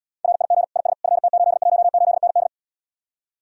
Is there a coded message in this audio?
Yes: Morse code from 0.2 to 2.5 s, heard as a keyed tone, about 77 wpm, 700 Hz. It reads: RIG IS FT818ND